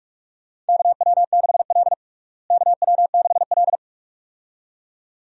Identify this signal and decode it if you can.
Morse code, heard as a keyed tone, about 46 wpm, 700 Hz. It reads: KW6L KW6L
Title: USA Call Signs - 4 Characters - Repeated 2x - 45wpm